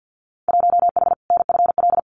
Morse code, heard as a keyed tone, about 50 wpm, 700 Hz. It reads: 1H NFL
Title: Winter Field Day Exchanges 50wpm